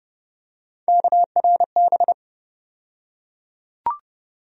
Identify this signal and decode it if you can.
Morse code, heard as a keyed tone, about 30 wpm, 700 Hz. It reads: KRB
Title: USA Call Sign Postfixes - 3 Letters 30wpm